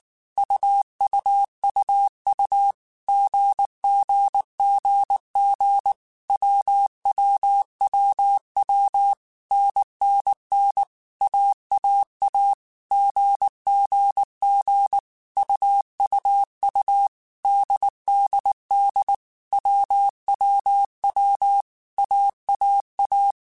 Morse code, heard as a keyed tone, about 19 wpm, 800 Hz. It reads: UUUU GGGG WWWW NNN AAA GGG UUU DDD WWW AAA
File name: Groups@18 / Groups@18 mp3 / AWUNDG@18